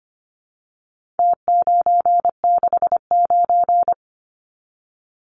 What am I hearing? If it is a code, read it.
Morse code, heard as a keyed tone, about 25 wpm, 700 Hz. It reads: T969
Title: Numbers - Contest Style - 4-Digit 25wpm